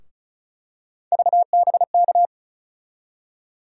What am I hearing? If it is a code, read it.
Morse code, heard as a keyed tone, about 36 wpm, 700 Hz. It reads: VBK
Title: USA Call Sign Postfixes - 3 Letters 35wpm